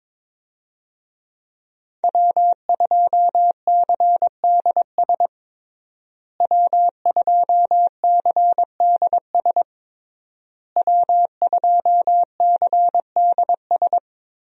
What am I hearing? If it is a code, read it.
Morse code, heard as a keyed tone, about 22 wpm, 700 Hz. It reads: W2CDS W2CDS W2CDS
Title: USA Call Signs - 5 Characters - Repeated 3x - 22wpm